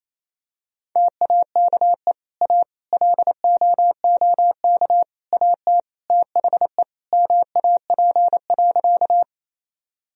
Morse code, heard as a keyed tone, about 28 wpm, 700 Hz. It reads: TAKE A LOOK AT THE MAP.
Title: Sentences from Top 1000 Words - Review Part 2 - 28wpm